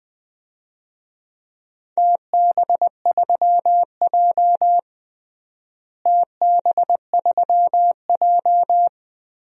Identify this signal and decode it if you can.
Morse code, heard as a keyed tone, about 20 wpm, 700 Hz. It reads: TB3J TB3J
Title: DXCC-100 Call Signs - 4 Characters - Repeated 3x - 20wpm